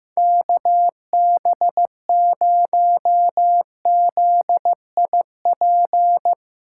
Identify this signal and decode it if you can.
Morse code, heard as a keyed tone, about 15 wpm, 700 Hz. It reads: KB0ZIP
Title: Speed Racing - USA Call Signs 15wpm